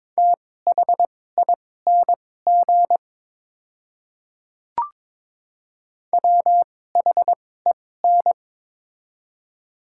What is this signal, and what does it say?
Morse code, heard as a keyed tone, about 23 wpm, spaced as for 16 wpm, 700 Hz. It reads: THING WHEN
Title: 2x Character Spacing - Top 200 Words - 22wpm